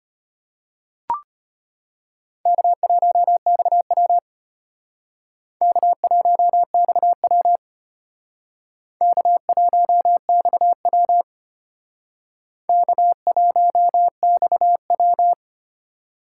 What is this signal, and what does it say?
Morse code, heard as a keyed tone, about 32 wpm, 700 Hz. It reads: K1XW K1XW K1XW K1XW